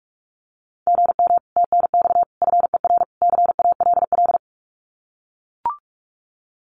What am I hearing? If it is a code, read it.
Morse code, heard as a keyed tone, about 44 wpm, 700 Hz. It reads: GM TNX FER CALL
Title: Common QSO Phrases Encore 45wpm